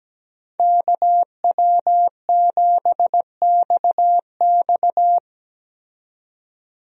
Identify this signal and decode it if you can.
Morse code, heard as a keyed tone, about 17 wpm, 700 Hz. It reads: KW7XX
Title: USA Call Signs - 5 Characters - 17wpm